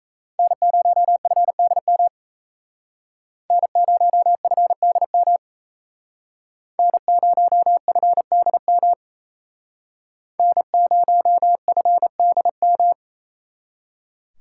Morse code, heard as a keyed tone, about 34 wpm, 700 Hz. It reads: N0FDM N0FDM N0FDM N0FDM
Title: Speed Racing - USA 1x3 Call Signs 28wpm